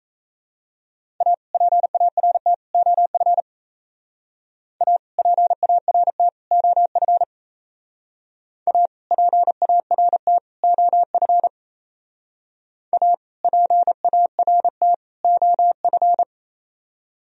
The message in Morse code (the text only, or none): A PART OF A PART OF A PART OF A PART OF